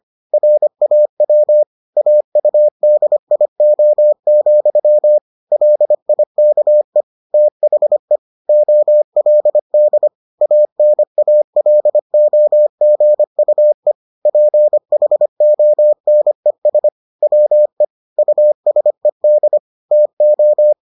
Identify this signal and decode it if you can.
Morse code, heard as a keyed tone, about 25 wpm, 600 Hz. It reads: RAW AUDIO, LIKE THE OLD ANALOGUE PHONES WE USED TO